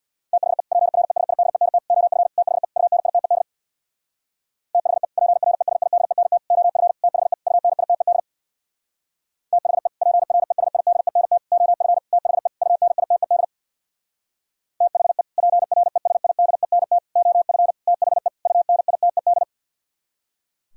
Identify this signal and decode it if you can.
Morse code, heard as a keyed tone, about 64 wpm, 700 Hz. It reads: THE PRESIDENT OF THE UNITED THE PRESIDENT OF THE UNITED THE PRESIDENT OF THE UNITED THE PRESIDENT OF THE UNITED